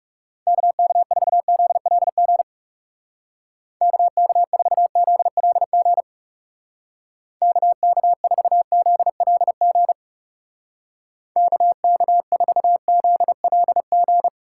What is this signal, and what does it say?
Morse code, heard as a keyed tone, about 37 wpm, 700 Hz. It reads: KK4ZLG KK4ZLG KK4ZLG KK4ZLG
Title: Speed Racing - USA 2x3 Call Signs 30wpm